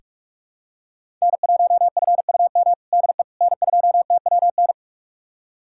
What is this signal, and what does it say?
Morse code, heard as a keyed tone, about 44 wpm, 700 Hz. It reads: N1FUM DE N2TWN
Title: USA Callsign DE Callsign - 5 Characters - 45wpm